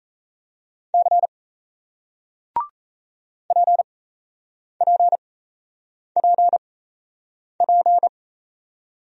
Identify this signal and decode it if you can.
Morse code, heard as a keyed tone, about 36 wpm, spaced as for 8 wpm, 700 Hz. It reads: C PPPP